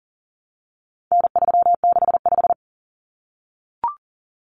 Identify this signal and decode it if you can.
Morse code, heard as a keyed tone, about 40 wpm, 700 Hz. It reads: N365
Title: Common Cut Numbers - 4-Digit 40wpm